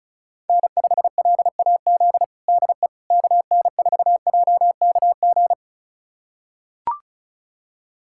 Morse code, heard as a keyed tone, about 36 wpm, 700 Hz. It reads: N5LAZ DE KN4JKG
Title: USA Callsign DE Callsign 35wpm